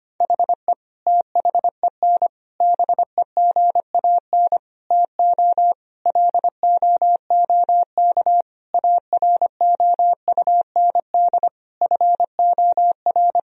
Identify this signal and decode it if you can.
Morse code, heard as a keyed tone, about 25 wpm, 700 Hz. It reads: HE THEN BEGAN TO LOOK AROUND FOR
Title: Sentences from Top 1000 Words - Review Part 3 - 25wpm